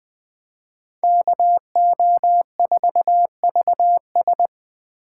Morse code, heard as a keyed tone, about 24 wpm, 700 Hz. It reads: KO4VS
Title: Speed Racing - USA Call Signs 20wpm